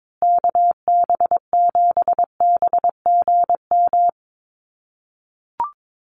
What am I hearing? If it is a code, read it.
Morse code, heard as a keyed tone, about 22 wpm, 700 Hz. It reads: KB7BGM
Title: USA Call Signs Repeated 3x 22wpm